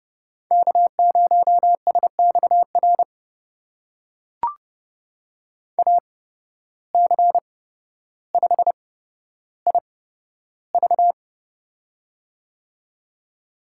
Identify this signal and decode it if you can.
Morse code, heard as a keyed tone, about 30 wpm, spaced as for 13 wpm, 700 Hz. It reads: K0SXR AC5IV
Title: USA Call Signs - 5 Characters - 8x Character Spacing - 30wpm